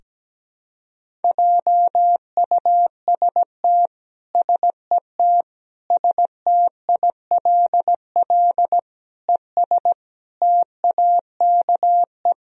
Morse code, heard as a keyed tone, about 17 wpm, 700 Hz. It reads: JUST SET STILL ES TAKE
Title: Sentences from Top 300 Words and Abbreviations 17wpm